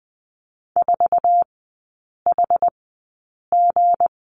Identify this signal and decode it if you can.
Morse code, heard as a keyed tone, about 19 wpm, spaced as for 8 wpm, 700 Hz. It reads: 4HG
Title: 184 Lesson 30e - Letter b - Three Characters 20wpm